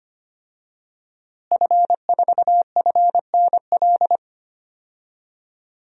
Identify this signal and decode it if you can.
Morse code, heard as a keyed tone, about 25 wpm, 700 Hz. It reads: F4FNL
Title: DXCC-100 Call Signs - 5 Characters - Repeated 3x - 25wpm